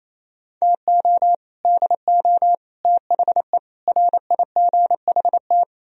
Morse code, heard as a keyed tone, about 37 wpm, 700 Hz. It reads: TO DO THE RIGHT